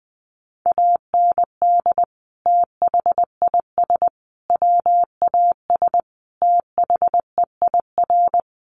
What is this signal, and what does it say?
Morse code, heard as a keyed tone, about 20 wpm, 700 Hz. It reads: AND THIS WAS THEIR
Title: Sentences from Top 1100 Words 20wpm